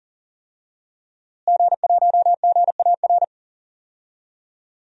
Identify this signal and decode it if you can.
Morse code, heard as a keyed tone, about 40 wpm, 700 Hz. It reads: G1GAR
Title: DXCC-100 Call Signs - 5 Characters - Repeated 2x - 40wpm